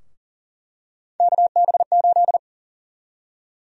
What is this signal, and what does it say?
Morse code, heard as a keyed tone, about 40 wpm, 700 Hz. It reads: KB8
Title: USA Call Sign Prefixes - 40wpm